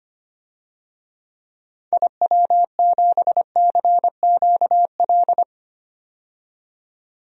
Morse code, heard as a keyed tone, about 25 wpm, 700 Hz. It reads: IW7CQL
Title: DXCC-100 Call Signs - 6 Characters - 25wpm